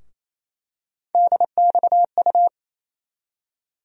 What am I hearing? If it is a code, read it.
Morse code, heard as a keyed tone, about 28 wpm, 700 Hz. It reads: DXU
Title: USA Call Sign Postfixes - 3 Letters 28wpm